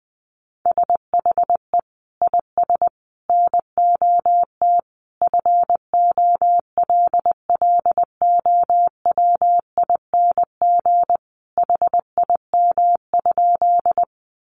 Morse code, heard as a keyed tone, about 20 wpm, 700 Hz. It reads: SHE IS NOT FOLLOWING HIM?
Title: Sentences from Top 700 Words 20wpm